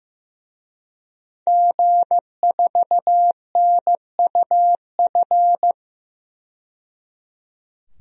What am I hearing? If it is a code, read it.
Morse code, heard as a keyed tone, about 15 wpm, 700 Hz. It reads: G4NUF
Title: DXCC-100 Call Signs - 5 Characters - 15wpm